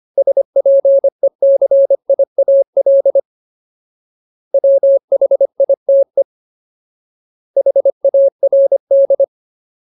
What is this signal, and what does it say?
Morse code, heard as a keyed tone, about 25 wpm, 550 Hz. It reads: SPECIAL WHITE HARD
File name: Ultimate Word List / 25WPM Ultimate